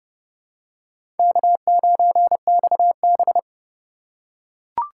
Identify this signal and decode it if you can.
Morse code, heard as a keyed tone, about 29 wpm, 700 Hz. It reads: K9XB
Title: USA Call Signs - 4 Characters - 2x Character Spacing - 30wpm